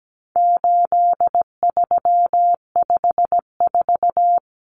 Morse code, heard as a keyed tone, about 17 wpm, 700 Hz. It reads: 8354
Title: Numbers - 4-Digit 17wpm